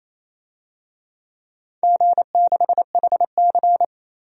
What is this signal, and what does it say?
Morse code, heard as a keyed tone, about 28 wpm, 700 Hz. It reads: G6HC
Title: DXCC-100 Call Signs - 4 Characters - 28wpm